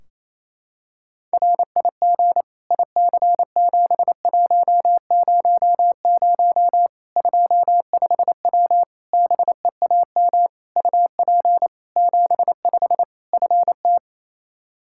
Morse code, heard as a keyed tone, about 28 wpm, 700 Hz. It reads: RIG IC7100 25W BEAM UP 75 FT
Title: Practice QSO - Exchange 2 - Info - 28wpm